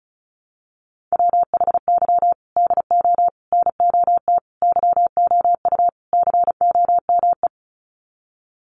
Morse code, heard as a keyed tone, about 34 wpm, 700 Hz. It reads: WHY DO NOT YOU COME
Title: Sets of 5 Words Encore2 35wpm